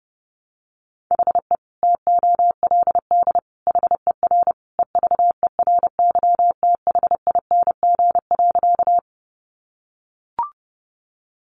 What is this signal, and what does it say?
Morse code, heard as a keyed tone, about 28 wpm, 700 Hz. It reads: HE TOLD HER EVERYTHING.
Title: Sentences from Top 700 Words 30wpm